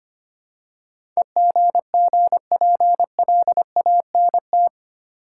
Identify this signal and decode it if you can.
Morse code, heard as a keyed tone, about 25 wpm, 700 Hz. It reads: EGGPLANT
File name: practice / 25wpm / Easy Words with Hints - 25wpm